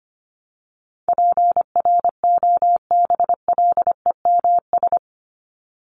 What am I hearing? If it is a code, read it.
Morse code, heard as a keyed tone, about 25 wpm, 700 Hz. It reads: PROBLEMS
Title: ES:S Suffix - Top 100 Words with Base Words 25wpm